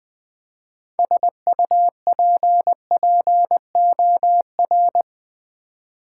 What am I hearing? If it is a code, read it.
Morse code, heard as a keyed tone, about 20 wpm, 700 Hz. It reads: SUPPOR
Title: Build A Word - 3 Letters At A Time - 9-Letter Words - 20wpm